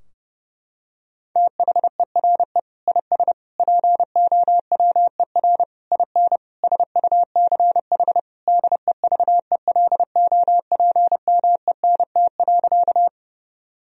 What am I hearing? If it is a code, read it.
Morse code, heard as a keyed tone, about 30 wpm, 700 Hz. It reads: THERE IS POWER IN SUCH DEVELOPMENT.
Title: Sentences from Top 1000 Words - Review Part 4 - 30wpm